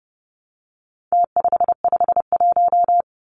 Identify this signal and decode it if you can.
Morse code, heard as a keyed tone, about 30 wpm, 700 Hz. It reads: T551
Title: Numbers - Contest Style - 4-Digit 30wpm